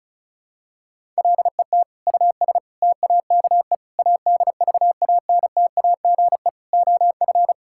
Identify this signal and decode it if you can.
Morse code, heard as a keyed tone, about 34 wpm, 700 Hz. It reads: LET US TAKE ADVANTAGE OF
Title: Sentences from Top 1000 Words - Review Part 6 - 35wpm